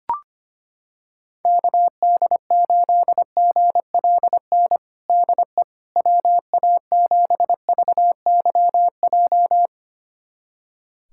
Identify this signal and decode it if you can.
Morse code, heard as a keyed tone, about 25 wpm, 700 Hz. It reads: KD8GLN DE WA7VYJ
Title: USA Callsign DE Callsign 25wpm